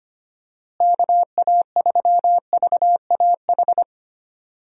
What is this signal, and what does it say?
Morse code, heard as a keyed tone, about 25 wpm, 700 Hz. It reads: KA3VAH
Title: USA Call Signs - 6 Characters - Repeated 2x - 25wpm